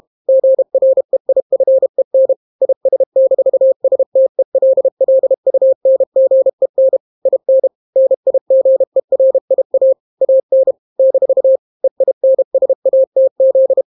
Morse code, heard as a keyed tone, about 32 wpm, 530 Hz. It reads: GREIFEN IS-STELLUNGEN IN NIGERIA AN - EINSATZ